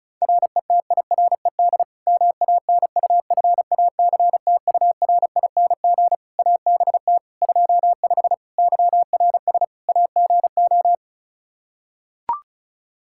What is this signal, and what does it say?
Morse code, heard as a keyed tone, about 34 wpm, 700 Hz. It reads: RETIRED MANUFACTURING ABT 25 YRS AGO
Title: K1USN SST QSO Practice 35wpm